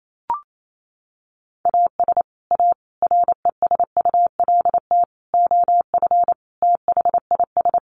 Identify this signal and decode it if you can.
Morse code, heard as a keyed tone, about 28 wpm, 700 Hz. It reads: AS A RESULT OF THIS
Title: Sets of 5 Words 28wpm